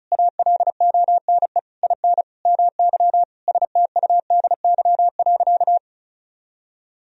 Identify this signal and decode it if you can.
Morse code, heard as a keyed tone, about 36 wpm, 700 Hz. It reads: ALONE IN MY STUDY.
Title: Sentences from Top 1000 Words - Review Part 1 - 35wpm